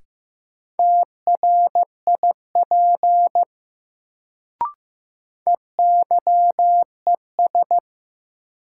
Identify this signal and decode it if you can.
Morse code, heard as a keyed tone, about 15 wpm, 700 Hz. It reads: TRIP EYES
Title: Sentences from Top 1000 Words - Review Part 4 - New Word Drill - 15wpm